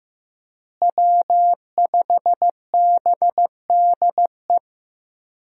Morse code, heard as a keyed tone, about 15 wpm, 700 Hz. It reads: W5BDE
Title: USA Call Signs - 5 Characters - Repeated 2x - 15wpm